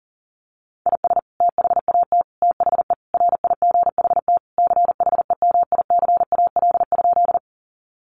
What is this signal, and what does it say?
Morse code, heard as a keyed tone, about 40 wpm, 700 Hz. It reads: IS THAT THE RIGHT CHEMICAL?
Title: Sentences from Top 1200 Words 40wpm